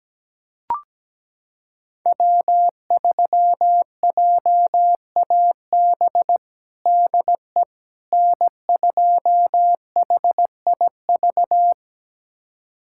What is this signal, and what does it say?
Morse code, heard as a keyed tone, about 17 wpm, 700 Hz. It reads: W3JAB DE N2HIV
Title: USA Callsign DE Callsign - 5 Characters - 17wpm